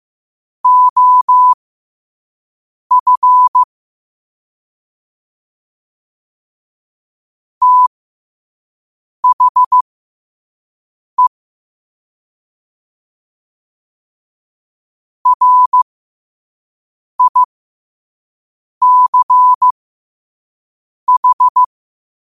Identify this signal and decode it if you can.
Morse code, heard as a keyed tone, about 15 wpm, spaced as for 5 wpm, 1000 Hz. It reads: OF THE RICH
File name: qotdfiles / 5wpm